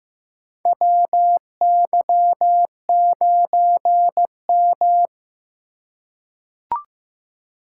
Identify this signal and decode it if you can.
Morse code, heard as a keyed tone, about 16 wpm, 700 Hz. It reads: WY9M
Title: USA Call Signs - 4 Characters - 2x Character Spacing - 15wpm